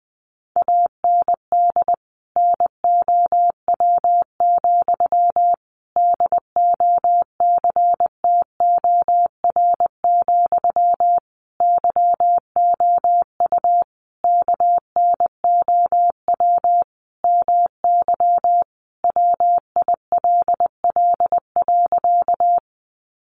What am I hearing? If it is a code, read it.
Morse code, heard as a keyed tone, about 20 wpm, 700 Hz. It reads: AND NOW, DOCTOR, YOU KNOW MY WILL.
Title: Sentences from Top 1500 Words 20wpm